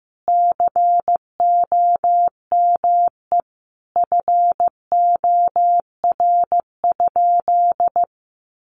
Morse code, heard as a keyed tone, about 15 wpm, 700 Hz. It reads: COME FOR?
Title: Sentences from Top 100 Words 15wpm